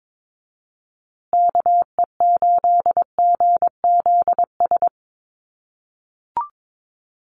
Morse code, heard as a keyed tone, about 22 wpm, 700 Hz. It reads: KE8GZS
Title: USA Call Signs Repeated 3x 22wpm